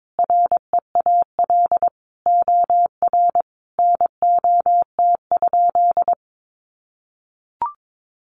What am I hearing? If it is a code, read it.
Morse code, heard as a keyed tone, about 22 wpm, 700 Hz. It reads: REAL OR NOT?
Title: Sentences from Top 600 Words 22wpm